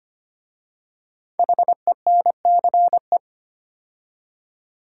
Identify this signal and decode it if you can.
Morse code, heard as a keyed tone, about 25 wpm, 700 Hz. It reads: HENCE
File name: 5 Letter Words - First 3 Letters Given - 25wpm